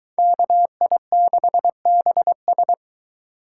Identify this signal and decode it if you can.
Morse code, heard as a keyed tone, about 23 wpm, 700 Hz. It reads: KI6BS